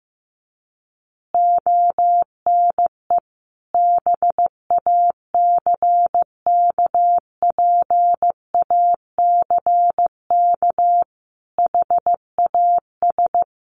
Morse code, heard as a keyed tone, about 15 wpm, 700 Hz. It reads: ONE BACKPACK HAS
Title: Additional Easy Phrases 15wpm